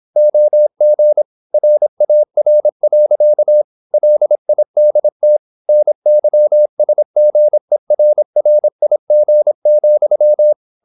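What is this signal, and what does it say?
Morse code, heard as a keyed tone, about 26 wpm, 600 Hz. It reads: OG RAR. LIDT NYSGERRIG,
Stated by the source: High-speed (QRQ) Morse at 26 wpm.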